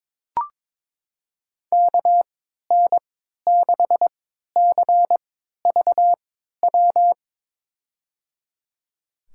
Morse code, heard as a keyed tone, about 22 wpm, spaced as for 13 wpm, 700 Hz. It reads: KN6CVW
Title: USA Call Signs - 3x Character Spacing - 22wpm